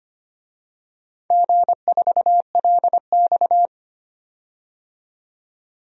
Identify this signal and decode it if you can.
Morse code, heard as a keyed tone, about 25 wpm, 700 Hz. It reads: G4LX
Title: DXCC-100 Call Signs - 4 Characters - 25wpm